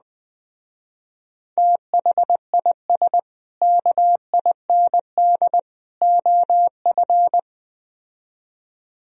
Morse code, heard as a keyed tone, about 20 wpm, 700 Hz. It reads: THIS KIND OF